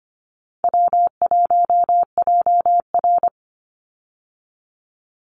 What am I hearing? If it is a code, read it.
Morse code, heard as a keyed tone, about 25 wpm, 700 Hz. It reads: W1JR
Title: Easy Call Signs Repeated 3x - 25wpm